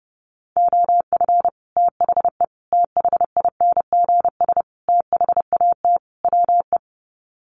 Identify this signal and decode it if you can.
Morse code, heard as a keyed tone, about 28 wpm, 700 Hz. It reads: OF THE THINGS THAT WE
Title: Sets of 5 Words Encore 30wpm